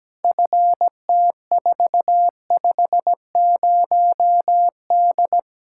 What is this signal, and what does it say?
Morse code, heard as a keyed tone, about 17 wpm, 700 Hz. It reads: FT450D
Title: Practice QSO - Exchange 2 - Info - 17wpm